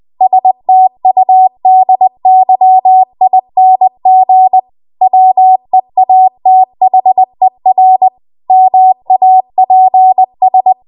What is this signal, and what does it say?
Morse code, heard as a keyed tone, about 20 wpm, 750 Hz. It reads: STUDYING WEATHER MAPS
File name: Archive / 20 WPM / 20WPM